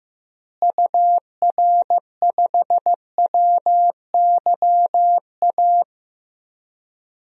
Morse code, heard as a keyed tone, about 15 wpm, 700 Hz. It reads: UR5WYA
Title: DXCC-100 Call Signs - 6 Characters - Repeated 2x - 15wpm